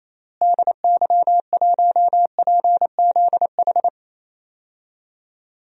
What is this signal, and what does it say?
Morse code, heard as a keyed tone, about 28 wpm, 700 Hz. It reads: DY1PZH
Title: DXCC-100 Call Signs - 6 Characters - Repeated 3x - 28wpm